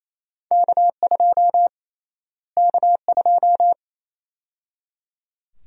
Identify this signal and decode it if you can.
Morse code, heard as a keyed tone, about 28 wpm, 700 Hz. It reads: K2 K2
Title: USA Call Sign Prefixes - Repeated 2x - 28wpm